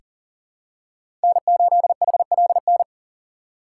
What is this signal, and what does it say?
Morse code, heard as a keyed tone, about 40 wpm, 700 Hz. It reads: N8HLN
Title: USA Call Signs - 4x Character Spacing - 40wpm